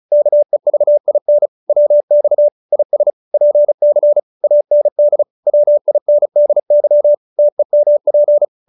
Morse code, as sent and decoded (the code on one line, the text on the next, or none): -.- . ...- .. -. / .-- -..- / .. ... / .--. -.-. / .- -. -.. / .-- .. -. -.. -.-- / - . -- .--.
KEVIN WX IS PC AND WINDY TEMP